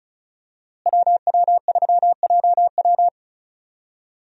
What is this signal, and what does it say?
Morse code, heard as a keyed tone, about 34 wpm, 700 Hz. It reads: WW3JW
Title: USA Call Signs - 5 Characters - Repeated 3x - 35wpm